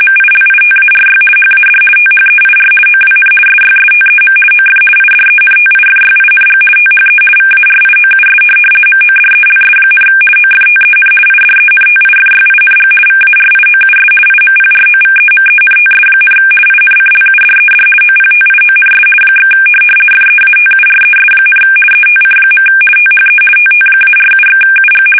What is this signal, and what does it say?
RTTY, about 50 baud. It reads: MILITARY HAD THEIR OWN SYSTEM OF IDENTIFYING THE VARIOUS MODELS, OFTEN IDENTIFYING VARIOUS IMPROVEMENTS, INCLUDED OPTIONS / FEATURES, ETC. THE TT-47/UG
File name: fsk850-50B / teletype-corp